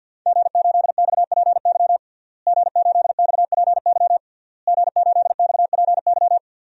Morse code, heard as a keyed tone, about 52 wpm, 700 Hz. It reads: G8XPY G8XPY G8XPY
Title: DXCC-100 Call Signs - 5 Characters - Repeated 3x - 50wpm